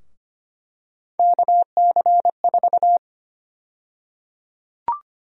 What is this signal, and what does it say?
Morse code, heard as a keyed tone, about 25 wpm, 700 Hz. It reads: KC4
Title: USA Call Sign Prefixes - Repeated 2x - 25wpm